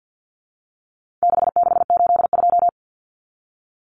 Morse code, heard as a keyed tone, about 50 wpm, 700 Hz. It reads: 6682
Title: Numbers - 4-Digit 50wpm